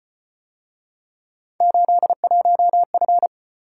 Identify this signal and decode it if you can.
Morse code, heard as a keyed tone, about 32 wpm, 700 Hz. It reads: 81F